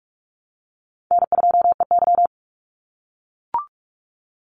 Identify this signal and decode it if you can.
Morse code, heard as a keyed tone, about 44 wpm, 700 Hz. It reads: N2EY
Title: Easy Call Signs - 45wpm